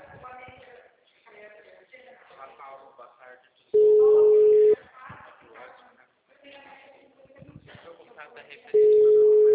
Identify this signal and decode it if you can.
Ringback tone (Europe, 425 Hz), with speech.
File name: out- -2010- - - .410